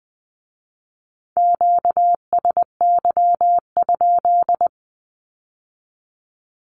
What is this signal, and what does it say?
Morse code, heard as a keyed tone, about 20 wpm, 700 Hz. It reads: QSY?
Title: Q-Codes with Spoken Meaning - 20wpm